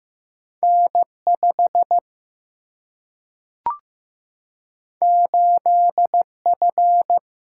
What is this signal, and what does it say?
Morse code, heard as a keyed tone, about 15 wpm, 700 Hz. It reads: N 5 8 F
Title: Single-Letter Single-Number 15wpm